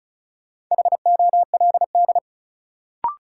Morse code, heard as a keyed tone, about 36 wpm, 700 Hz. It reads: HOLD
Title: Sentences from Top 400 Words - New Word Drill - 35wpm